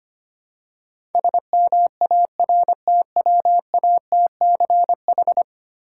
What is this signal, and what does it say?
Morse code, heard as a keyed tone, about 25 wpm, 700 Hz. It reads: SMARTWATCH
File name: practice / 25wpm / Difficult Words with Hints - 25wpm